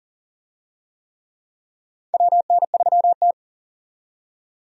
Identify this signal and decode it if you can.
Morse code, heard as a keyed tone, about 40 wpm, 700 Hz. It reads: WN3T
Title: USA Call Signs - 4 Characters - 40wpm